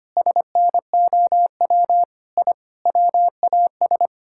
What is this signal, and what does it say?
Morse code, heard as a keyed tone, about 25 wpm, 700 Hz. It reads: SNOW I WAS
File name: practice / 25wpm / Sentences with 4-Word Phrases - 25wpm